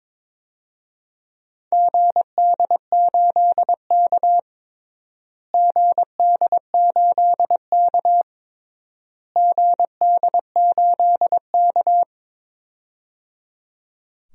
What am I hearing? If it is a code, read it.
Morse code, heard as a keyed tone, about 22 wpm, 700 Hz. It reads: GD8K GD8K GD8K
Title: DXCC-100 Call Signs - 4 Characters - Repeated 3x - 22wpm